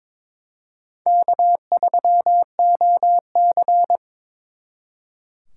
Morse code, heard as a keyed tone, about 26 wpm, 700 Hz. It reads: K3OC
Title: Speed Racing - USA 1x2 Call Signs 22wpm